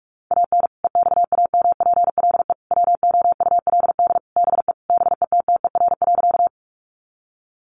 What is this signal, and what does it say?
Morse code, heard as a keyed tone, about 44 wpm, 700 Hz. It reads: AN EXAMPLE WOULD BE BETTER.
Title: Sentences from Top 400 Words 45wpm